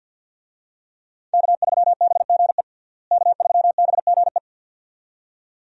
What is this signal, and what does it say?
Morse code, heard as a keyed tone, about 46 wpm, 700 Hz. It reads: K3BGE K3BGE
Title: USA Call Signs - 5 Characters - Repeated 2x - 50wpm